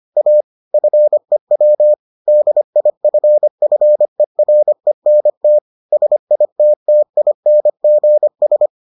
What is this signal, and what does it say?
Morse code, heard as a keyed tone, about 25 wpm, 600 Hz. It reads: A FEW DIFFERENT SITTINGS